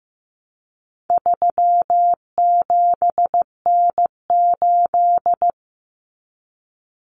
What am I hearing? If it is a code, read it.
Morse code, heard as a keyed tone, about 15 wpm, 700 Hz. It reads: 37N8
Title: Common Cut Numbers - 4-Digit 15wpm